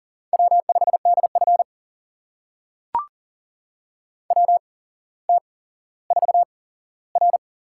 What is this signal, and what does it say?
Morse code, heard as a keyed tone, about 43 wpm, spaced as for 17 wpm, 700 Hz. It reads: W5DF WT4R
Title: USA Call Signs - 4 Characters - 8x Character Spacing - 40wpm